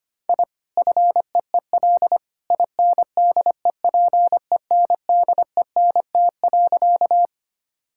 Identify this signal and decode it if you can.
Morse code, heard as a keyed tone, about 25 wpm, 700 Hz. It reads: I FEEL INDEPENDENT.
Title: Sentences from Top 1000 Words - Review Part 6 - 25wpm